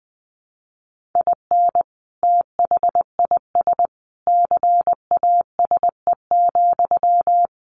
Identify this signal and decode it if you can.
Morse code, heard as a keyed tone, about 20 wpm, 700 Hz. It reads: IN THIS CASE,
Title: Sentences from Top 1500 Words 20wpm